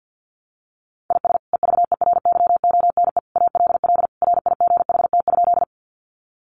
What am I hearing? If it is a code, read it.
Morse code, heard as a keyed tone, about 50 wpm, 700 Hz. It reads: IS EVERYONE ALL RIGHT?
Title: Sentences from Top 800 Words 50wpm